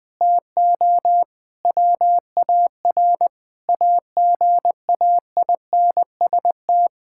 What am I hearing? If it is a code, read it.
Morse code, heard as a keyed tone, about 21 wpm, 700 Hz. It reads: TO WAR AGAINST
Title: Sentences from Top 1000 Words - Review Part 5 - 20wpm